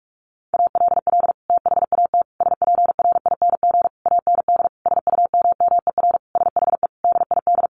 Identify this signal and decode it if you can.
Morse code, heard as a keyed tone, about 44 wpm, 700 Hz. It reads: ALL THAT SPRING AND SUMMER SHE DID
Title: Sentences from Top 1200 Words 45wpm